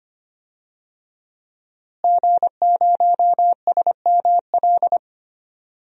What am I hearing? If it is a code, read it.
Morse code, heard as a keyed tone, about 25 wpm, 700 Hz. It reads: G0SML
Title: DXCC-100 Call Signs - 5 Characters - Repeated 2x - 25wpm